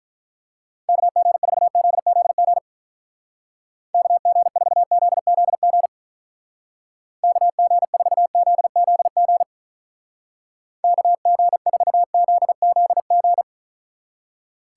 Morse code, heard as a keyed tone, about 44 wpm, 700 Hz. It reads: KG4ZZG KG4ZZG KG4ZZG KG4ZZG